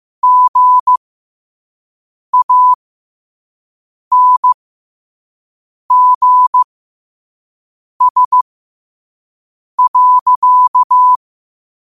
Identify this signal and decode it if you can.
Morse code, heard as a keyed tone, about 15 wpm, spaced as for 5 wpm, 1000 Hz. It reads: GANGS.